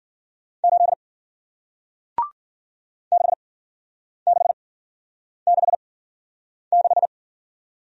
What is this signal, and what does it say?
Morse code, heard as a keyed tone, about 55 wpm, spaced as for 9 wpm, 700 Hz. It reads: 8 6666